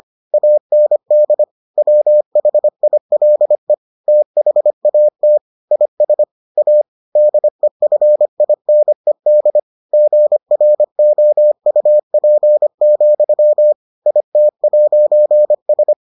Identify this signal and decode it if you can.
Morse code, heard as a keyed tone, about 25 wpm, 600 Hz. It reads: AND WHILE THAT IS A DEFINED GROUP, IT'S